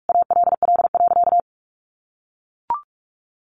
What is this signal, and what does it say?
Morse code, heard as a keyed tone, about 44 wpm, 700 Hz. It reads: ALL.
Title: Sentences from Top 800 Words 45wpm